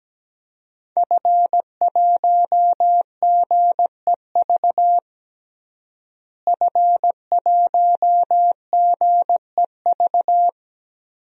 Morse code, heard as a keyed tone, about 17 wpm, 700 Hz. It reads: F1GEV F1GEV
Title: DXCC-100 Call Signs - 5 Characters - Repeated 3x - 17wpm